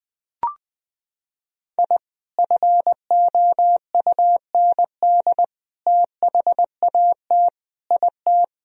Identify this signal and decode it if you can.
Morse code, heard as a keyed tone, about 20 wpm, 700 Hz. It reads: I FOUND THAT IT
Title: Sentences from Top 1000 Words - Review Part 1 - 20wpm